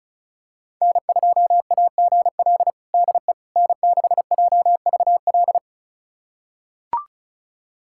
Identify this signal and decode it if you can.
Morse code, heard as a keyed tone, about 34 wpm, 700 Hz. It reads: N2AGL DE N6JVL
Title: USA Callsign DE Callsign - 5 Characters - 35wpm